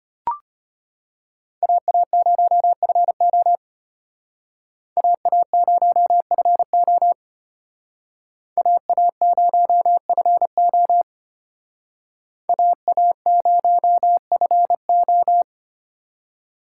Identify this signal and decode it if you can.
Morse code, heard as a keyed tone, about 31 wpm, 700 Hz. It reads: AA0FO AA0FO AA0FO AA0FO
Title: Speed Racing - USA 2x2 Call Signs 25wpm